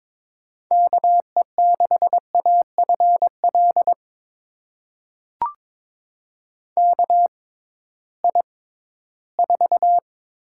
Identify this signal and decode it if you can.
Morse code, heard as a keyed tone, about 22 wpm, spaced as for 14 wpm, 700 Hz. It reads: KE6AFL KI4
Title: USA Call Signs - 6 Characters - 6x Character Spacing - 22wpm